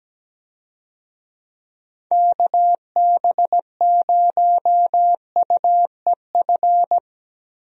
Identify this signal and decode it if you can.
Morse code, heard as a keyed tone, about 17 wpm, 700 Hz. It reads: KB0UEF
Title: USA Call Signs - 6 Characters - Repeated 2x - 17wpm